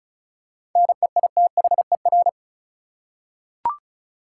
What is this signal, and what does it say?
Morse code, heard as a keyed tone, about 34 wpm, 700 Hz. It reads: NEITHER
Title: Seven Letter Words - 35wpm